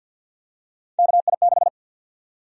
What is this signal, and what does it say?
Morse code, heard as a keyed tone, about 52 wpm, 700 Hz. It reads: KI6
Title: USA Call Sign Prefixes - 50wpm